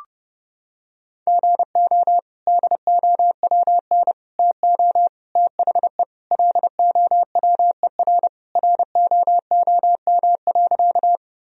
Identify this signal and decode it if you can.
Morse code, heard as a keyed tone, about 30 wpm, 700 Hz. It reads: GO DOWN TO THE LOWER ROOM.
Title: Sentences from Top 1000 Words - Review Part 4 - 30wpm